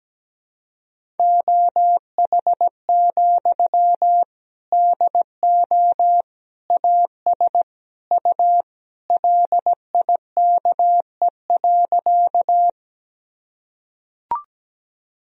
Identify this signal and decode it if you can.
Morse code, heard as a keyed tone, about 17 wpm, 700 Hz. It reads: OH, DO AS U LIKE.
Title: Sentences from Top 500 Words and Abbreviations 17wpm